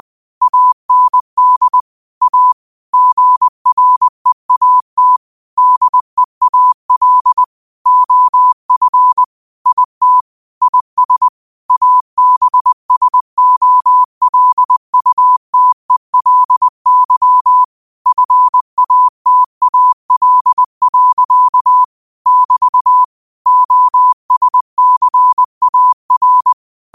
Morse code, heard as a keyed tone, about 20 wpm, 1000 Hz. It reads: AND A GREAT DEAL OF IT IS ABSOLUTELY FATAL. <BT> OSCAR
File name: qotdfiles / 20wpm